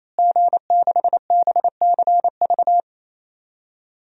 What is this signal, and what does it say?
Morse code, heard as a keyed tone, about 28 wpm, 690 Hz. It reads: G6BCV
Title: DXCC-100 Call Signs - 5 Characters - Repeated 3x - 28wpm